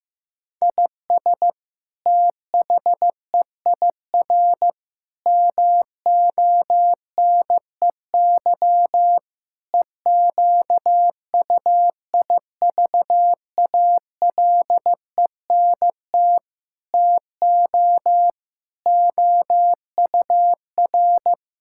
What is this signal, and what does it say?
Morse code, heard as a keyed tone, about 15 wpm, 700 Hz. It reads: IS THEIR MONEY EQUIVALENT TO OUR
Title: Sentences from Top 1600 Words 15wpm